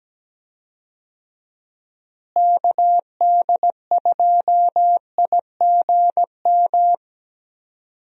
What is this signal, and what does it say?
Morse code, heard as a keyed tone, about 17 wpm, 700 Hz. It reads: KD2IGM
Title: USA Call Signs - 6 Characters - Repeated 2x - 17wpm